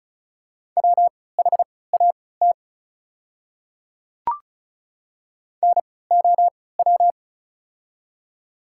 Morse code, heard as a keyed tone, about 36 wpm, spaced as for 20 wpm, 700 Hz. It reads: WHAT NOW
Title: 3x Character Spacing - Top 100 Words - 35wpm